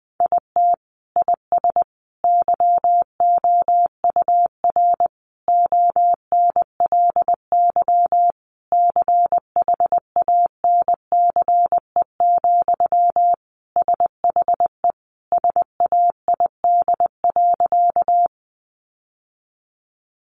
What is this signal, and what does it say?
Morse code, heard as a keyed tone, about 20 wpm, 700 Hz. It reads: IT IS YOUR ONLY CHANCE, SHE SAID.
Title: Sentences from Top 800 Words 20wpm